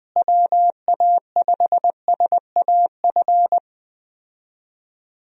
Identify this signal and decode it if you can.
Morse code, heard as a keyed tone, about 21 wpm, 700 Hz. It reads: WA5SAF
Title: USA Call Signs - 6 Characters - Repeated 2x - 20wpm